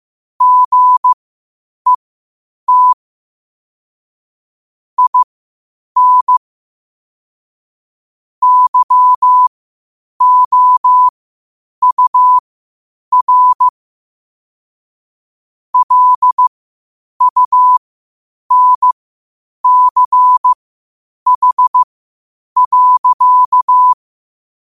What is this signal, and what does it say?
Morse code, heard as a keyed tone, about 15 wpm, spaced as for 8 wpm, 1000 Hz. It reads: GET IN YOUR LUNCH.
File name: qotdfiles / 8wpm